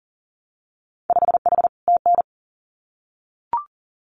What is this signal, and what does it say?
Morse code, heard as a keyed tone, about 41 wpm, 700 Hz. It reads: 5H TN
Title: Winter Field Day Exchanges Encore 40wpm